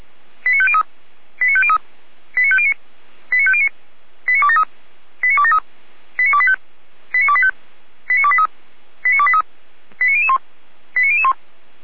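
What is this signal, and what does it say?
Five-tone selective call (ZVEI), heard as quick runs of five tones, tones of about 70 ms. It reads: 89503 89503 89509 89509 89273 89273 89285 89285 89283 89283 89001 89001
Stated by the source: ZVEI five-tone selective call received on an SDR, recorded by feuerwehr-orlamuende.org fire brigade.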